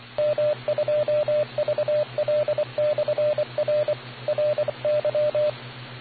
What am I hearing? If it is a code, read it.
Morse code, heard as a keyed tone, about 25 wpm, 600 Hz, with radio noise. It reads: M2VL/R RY